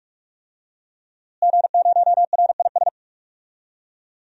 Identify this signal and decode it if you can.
Morse code, heard as a keyed tone, about 43 wpm, 700 Hz. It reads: G0RIS
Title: DXCC-100 Call Signs - 5 Characters - Repeated 3x - 45wpm